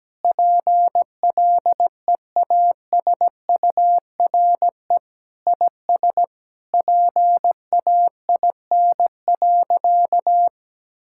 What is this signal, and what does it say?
Morse code, heard as a keyed tone, about 17 wpm, 700 Hz. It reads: PLEASURE IS PAIN.
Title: Sentences from Top 1600 Words 17wpm